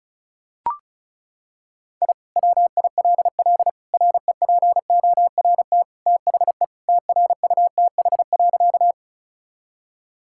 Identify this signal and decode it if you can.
Morse code, heard as a keyed tone, about 36 wpm, 700 Hz. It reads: I WILL REPORT THE TRUTH.
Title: Sentences from Top 1000 Words - Review Part 6 - 35wpm